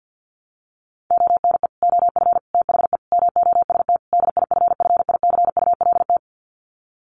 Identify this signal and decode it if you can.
Morse code, heard as a keyed tone, about 50 wpm, 700 Hz. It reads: ONE OF THE MOST DIFFICULT
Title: Sets of 5 Words Encore2 50wpm